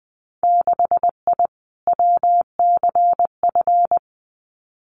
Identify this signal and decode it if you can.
Morse code, heard as a keyed tone, about 20 wpm, 700 Hz. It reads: 6I WCF
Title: Winter Field Day Exchanges Encore 20wpm